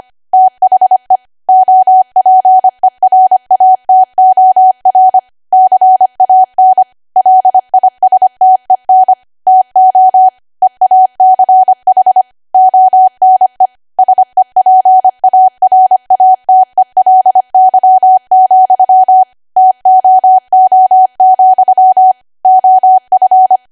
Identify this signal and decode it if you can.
Morse code, heard as a keyed tone, about 25 wpm, 750 Hz. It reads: THE OPERATOR CAN LISTEN TO EACH ONE SEPARATELY, TOO, OF
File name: Archive / 25 WPM / 25WPM